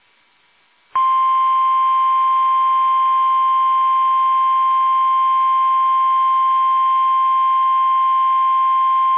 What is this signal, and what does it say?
Emergency Alert System (SAME) tones (the 1050 Hz attention tone). It is heard from 1.0 to 8.6 s.
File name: eas / KUSO / KUSO